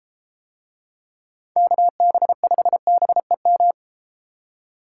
Morse code, heard as a keyed tone, about 33 wpm, 700 Hz. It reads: KB5BEM